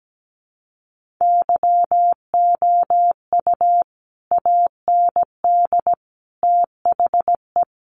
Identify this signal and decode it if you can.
Morse code, heard as a keyed tone, about 17 wpm, 700 Hz. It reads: YOU AND THE
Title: Sentences from Top 1300 Words 17wpm